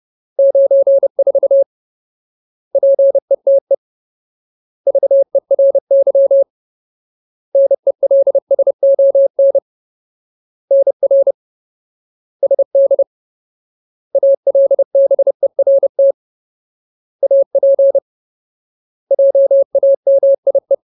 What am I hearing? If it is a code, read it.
Morse code, heard as a keyed tone, about 30 wpm, 550 Hz. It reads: 94 PETE VERY NELSON NR SD ALBERT AP JAMIE
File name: Ultimate Word List / 30WPM Ultimate